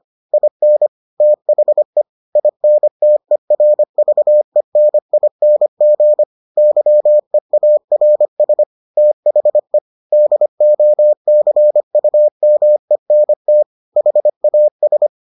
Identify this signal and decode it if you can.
Morse code, heard as a keyed tone, about 25 wpm, 600 Hz. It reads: IN THE INTERVENING YEARS THE DOCUMENT HAS